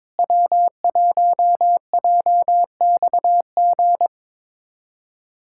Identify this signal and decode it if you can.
Morse code, heard as a keyed tone, about 21 wpm, 700 Hz. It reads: W1JXG
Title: USA Callsign DE Callsign 22wpm